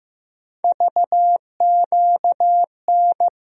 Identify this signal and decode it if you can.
Morse code, heard as a keyed tone, about 15 wpm, 700 Hz. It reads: VQN
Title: USA Call Sign Postfixes - 3 Letters 15wpm